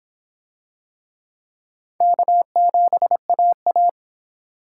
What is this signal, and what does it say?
Morse code, heard as a keyed tone, about 26 wpm, 700 Hz. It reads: K7AA